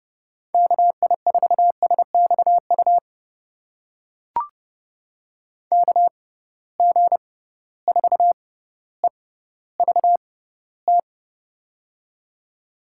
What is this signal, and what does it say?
Morse code, heard as a keyed tone, about 30 wpm, spaced as for 15 wpm, 700 Hz. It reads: KI4SXU KG4EVT
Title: USA Call Signs - 6x Character Spacing - 30wpm